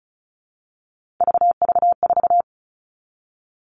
Morse code, heard as a keyed tone, about 34 wpm, 700 Hz. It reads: VV4
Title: All Cut Numbers - 3-Digit 35wpm